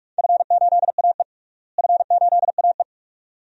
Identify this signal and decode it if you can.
Morse code, heard as a keyed tone, about 46 wpm, 700 Hz. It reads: F8AE F8AE
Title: DXCC-100 Call Signs - 4 Characters - Repeated 2x - 45wpm